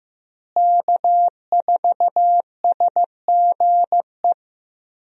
Morse code, heard as a keyed tone, about 15 wpm, 700 Hz. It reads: K4SGE
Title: USA Call Signs - 5 Characters - Repeated 3x - 15wpm